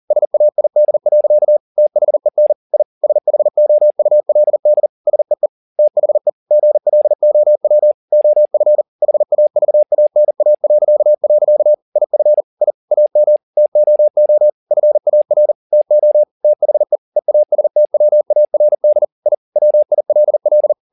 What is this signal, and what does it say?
Morse code, heard as a keyed tone, about 40 wpm, 600 Hz. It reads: SAID. THEN I SHOULD SEE THE GLOW OF HAVANA.. IF I AM TOO FAR TO THE EASTWARD I WILL
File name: the old man and the sea 2 40wpm